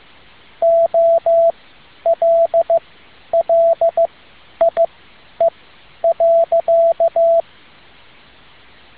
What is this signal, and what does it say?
Morse code, heard as a keyed tone, about 15 wpm, spaced as for 10 wpm, 670 Hz, with radio noise. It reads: OLLIE.